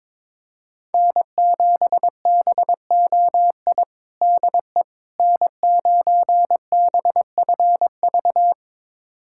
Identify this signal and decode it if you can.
Morse code, heard as a keyed tone, about 22 wpm, 700 Hz. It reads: N7BOI DE N9BFV
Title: USA Callsign DE Callsign - 5 Characters - 22wpm